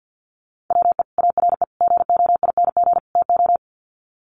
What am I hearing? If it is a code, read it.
Morse code, heard as a keyed tone, about 50 wpm, 700 Hz. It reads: WE ARE GOING TO
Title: Sets of 4 Words 50wpm